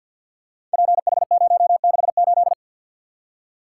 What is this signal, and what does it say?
Morse code, heard as a keyed tone, about 53 wpm, 700 Hz. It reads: PH068
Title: Code Groups - 5 Letter:Numbers - 50wpm